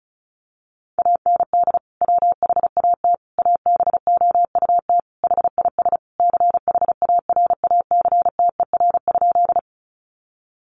Morse code, heard as a keyed tone, about 34 wpm, 700 Hz. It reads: AND WHAT ABOUT HIS CHARACTER?
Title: Sentences from Top 1200 Words 35wpm